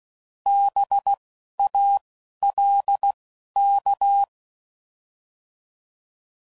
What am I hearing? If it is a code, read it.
Morse code, heard as a keyed tone, about 16 wpm, 800 Hz. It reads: B A L K